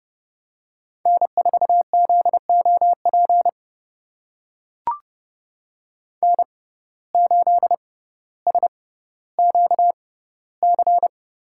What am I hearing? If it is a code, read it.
Morse code, heard as a keyed tone, about 31 wpm, spaced as for 16 wpm, 700 Hz. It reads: N4ZOP N8SQC
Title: USA Call Signs - 5 Characters - 6x Character Spacing - 30wpm